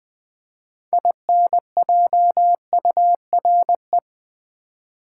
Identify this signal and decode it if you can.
Morse code, heard as a keyed tone, about 20 wpm, 700 Hz. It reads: INJURE
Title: 6 Letter Words - First 3 Letters Given - 20wpm